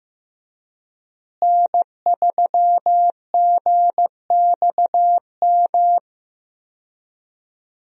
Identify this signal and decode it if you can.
Morse code, heard as a keyed tone, about 15 wpm, 700 Hz. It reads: N3GXM
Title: USA Call Signs - 5 Characters - Repeated 2x - 15wpm